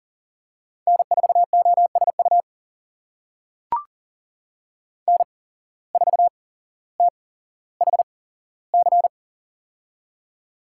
Morse code, heard as a keyed tone, about 38 wpm, spaced as for 17 wpm, 700 Hz. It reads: N4OSU N4THC
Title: USA Call Signs - 5 Characters - 8x Character Spacing - 40wpm